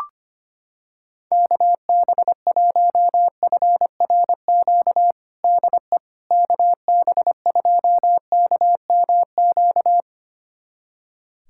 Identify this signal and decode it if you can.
Morse code, heard as a keyed tone, about 25 wpm, 700 Hz. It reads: KB1FRQ DE KB2KMQ
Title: USA Callsign DE Callsign - 6 Characters - 25wpm